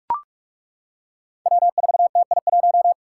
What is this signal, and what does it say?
Morse code, heard as a keyed tone, about 42 wpm, 700 Hz. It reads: W4TI1
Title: Five Challenging Letter-Number Combinations 45wpm